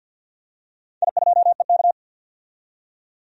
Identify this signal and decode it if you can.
Morse code, heard as a keyed tone, about 51 wpm, 700 Hz. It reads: I2EK
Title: DXCC-100 Call Signs - 4 Characters - 50wpm